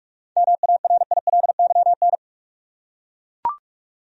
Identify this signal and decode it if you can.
Morse code, heard as a keyed tone, about 41 wpm, 700 Hz. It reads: MARILYN
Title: USA Common Names - 45wpm